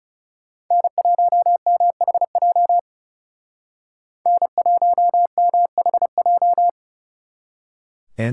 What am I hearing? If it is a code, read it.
Morse code, heard as a keyed tone, about 34 wpm, 700 Hz. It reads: N1MHJ N1MHJ
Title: Speed Racing - USA Call Signs 30wpm